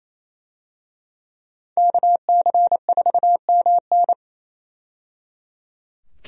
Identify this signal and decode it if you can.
Morse code, heard as a keyed tone, about 28 wpm, 700 Hz. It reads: KC4MN
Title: USA Call Signs - 5 Characters - 28wpm